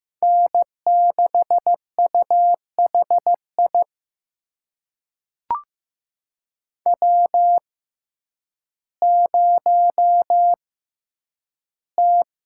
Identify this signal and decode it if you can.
Morse code, heard as a keyed tone, about 15 wpm, spaced as for 9 wpm, 700 Hz. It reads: N6UHI W0T
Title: USA Call Signs - 5 Characters - 6x Character Spacing - 15wpm